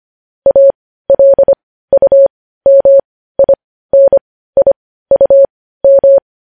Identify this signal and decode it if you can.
Morse code, heard as a keyed tone, about 25 wpm, spaced as for 15 wpm, 550 Hz. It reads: ALUMINIUM
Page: news headlines in morse code at 15 wpm